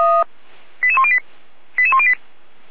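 Five-tone selective call (ZVEI), heard as quick runs of five tones, tones of about 70 ms. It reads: 88108 88108
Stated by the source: ZVEI five-tone selective call received on SDR.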